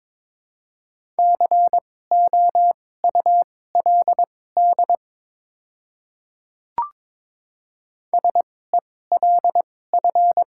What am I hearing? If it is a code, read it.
Morse code, heard as a keyed tone, about 22 wpm, spaced as for 16 wpm, 700 Hz. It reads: COULD SELF
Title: 2x Character Spacing - Top 300 Words - 22wpm